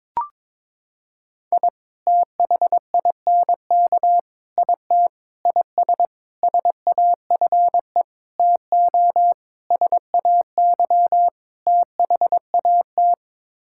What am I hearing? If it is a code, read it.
Morse code, heard as a keyed tone, about 22 wpm, 700 Hz. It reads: I THINK IT IS SAFE TO SAY THAT
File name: practice / 22wpm / Sets of 8 Words - 22wpm